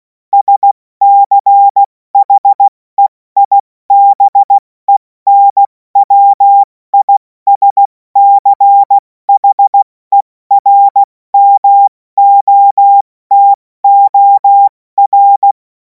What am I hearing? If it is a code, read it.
Morse code, heard as a keyed tone, about 16 wpm, 800 Hz. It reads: SCHEIBENWISCHERMOTOR
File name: Morsekurs / Witze / Witz 28. April 2026 70 F80